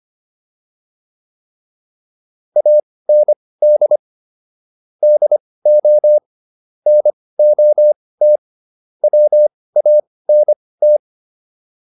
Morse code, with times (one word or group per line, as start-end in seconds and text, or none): 2.6-4.0 AND
5.0-6.2 DO
6.9-8.4 NOT
9.0-11.0 WANT